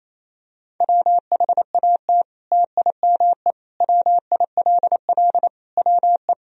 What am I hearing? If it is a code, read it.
Morse code, heard as a keyed tone, about 28 wpm, 700 Hz. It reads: WHAT TIME WILL WE
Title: Sentences from Top 1000 Words - Review Part 1 - 28wpm